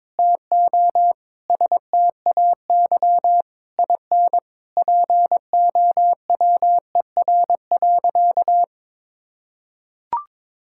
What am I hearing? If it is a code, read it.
Morse code, heard as a keyed tone, about 22 wpm, 700 Hz. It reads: TO STAY IN POWER.
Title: Sentences from Top 1000 Words - Review Part 5 - 22wpm